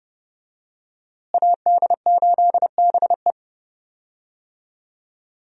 Morse code, heard as a keyed tone, about 30 wpm, 700 Hz. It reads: AD8BE
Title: USA Call Signs - 5 Characters - Repeated 2x - 30wpm